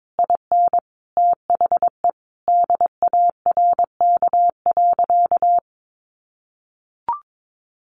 Morse code, heard as a keyed tone, about 22 wpm, 700 Hz. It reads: IN THE DARK.
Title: Sentences from Top 1100 Words 22wpm